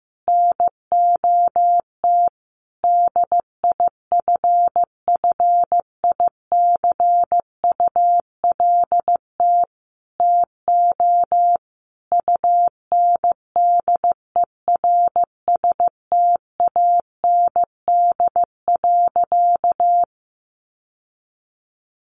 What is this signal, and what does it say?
Morse code, heard as a keyed tone, about 15 wpm, 700 Hz. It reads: NOT DIFFICULT TO UNDERSTAND.
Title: Sentences from Top 1100 Words 15wpm